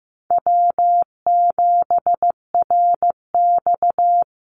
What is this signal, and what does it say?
Morse code, heard as a keyed tone, about 15 wpm, 700 Hz. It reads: W7RX
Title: Call Signs Repeated 2x - 15wpm